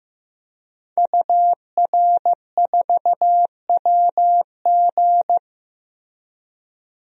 Morse code, heard as a keyed tone, about 15 wpm, 700 Hz. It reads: UR4WG
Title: DXCC-100 Call Signs - 5 Characters - 15wpm